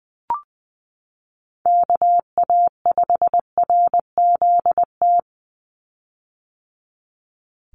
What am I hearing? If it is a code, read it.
Morse code, heard as a keyed tone, about 20 wpm, 700 Hz. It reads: KA5RZT
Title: USA Call Signs 20wpm